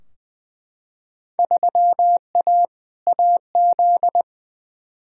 Morse code, heard as a keyed tone, about 21 wpm, 700 Hz. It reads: 3A AZ
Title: Field Day Exchanges - 20wpm